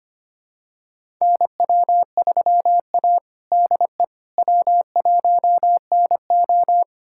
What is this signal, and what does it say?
Morse code, heard as a keyed tone, about 25 wpm, 700 Hz. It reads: NW3A DE W1NO
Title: USA Callsign DE Callsign - 4 Characters - 25wpm